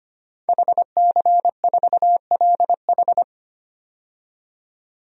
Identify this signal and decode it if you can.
Morse code, heard as a keyed tone, about 25 wpm, 700 Hz. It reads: HC4LH
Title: DXCC-100 Call Signs - 5 Characters - 25wpm